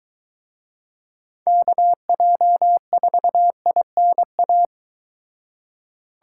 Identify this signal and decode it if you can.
Morse code, heard as a keyed tone, about 23 wpm, 700 Hz. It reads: KJ4INA